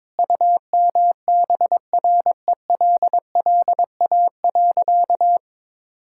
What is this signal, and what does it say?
Morse code, heard as a keyed tone, about 21 wpm, 700 Hz. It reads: UMBRELLA.
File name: practice / 22wpm / Sentences with 3-Word Phrases - 22wpm